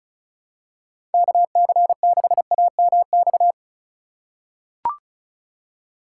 Morse code, heard as a keyed tone, about 36 wpm, 700 Hz. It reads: KC6AMX
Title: USA Call Signs - 6 Characters - 3x Character Spacing - 35wpm